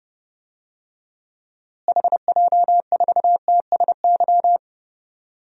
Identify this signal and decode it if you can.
Morse code, heard as a keyed tone, about 30 wpm, 700 Hz. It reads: HJ4TSY
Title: DXCC-100 Call Signs - 6 Characters - Repeated 3x - 30wpm